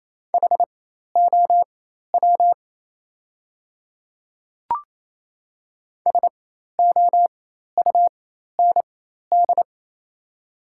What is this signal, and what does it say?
Morse code, heard as a keyed tone, about 28 wpm, spaced as for 13 wpm, 700 Hz. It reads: HOW SOUND
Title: 4x Character Spacing - Top 100 Words - 28wpm